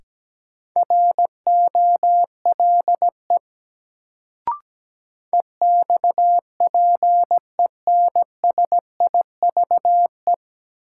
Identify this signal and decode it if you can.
Morse code, heard as a keyed tone, about 17 wpm, 700 Hz. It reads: ROLE EXPENSIVE
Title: Sentences from Top 1500 Words - New Word Drill Alt - 17wpm